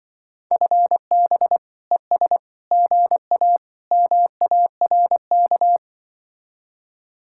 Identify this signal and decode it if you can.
Morse code, heard as a keyed tone, about 25 wpm, 700 Hz. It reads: FB ES GA MARK